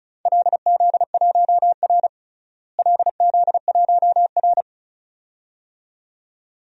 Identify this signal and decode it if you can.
Morse code, heard as a keyed tone, about 34 wpm, 700 Hz. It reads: LZ1R LZ1R
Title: DXCC-100 Call Signs - 4 Characters - Repeated 2x - 35wpm